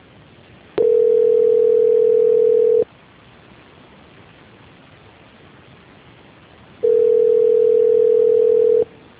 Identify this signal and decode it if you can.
Ringback tone (North America), 440 + 480 Hz.